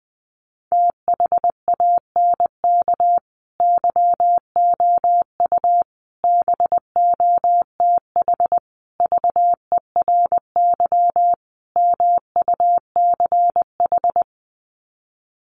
Morse code, heard as a keyed tone, about 20 wpm, 700 Hz. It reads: THANK YOU BOTH VERY MUCH
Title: Sets of 5 Words 20wpm